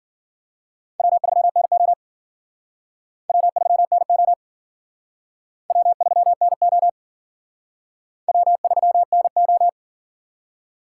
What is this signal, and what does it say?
Morse code, heard as a keyed tone, about 50 wpm, 700 Hz. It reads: W3NO W3NO W3NO W3NO
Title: Speed Racing - USA 1x2 Call Signs 40wpm